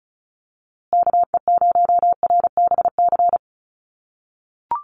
Morse code, heard as a keyed tone, about 34 wpm, 700 Hz. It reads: KE0RBC
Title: USA Call Signs Repeated 2x 35wpm